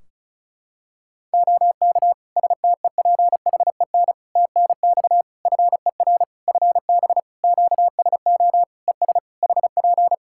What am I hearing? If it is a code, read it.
Morse code, heard as a keyed tone, about 34 wpm, 700 Hz. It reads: OK STEPHEN TNX FER FB QSO ES HP
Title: K1USN SST QSO Practice 35wpm